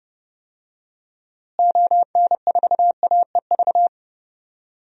Morse code, heard as a keyed tone, about 31 wpm, 700 Hz. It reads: ON4AEV
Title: DXCC-100 Call Signs - 6 Characters - Repeated 3x - 30wpm